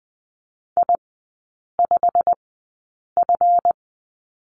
Morse code, heard as a keyed tone, about 19 wpm, spaced as for 8 wpm, 700 Hz. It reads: I5F
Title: 166 Lesson 27e - Number 7 - Three Characters 20wpm